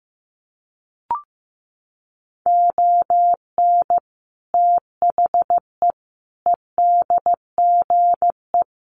Morse code, heard as a keyed tone, about 15 wpm, 700 Hz. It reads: ON THE EDGE
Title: Sets of 5 Words Encore2 15wpm